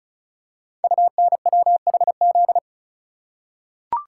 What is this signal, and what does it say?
Morse code, heard as a keyed tone, about 36 wpm, 700 Hz. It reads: UNWHZ
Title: Code Groups - 5 Letters - 35wpm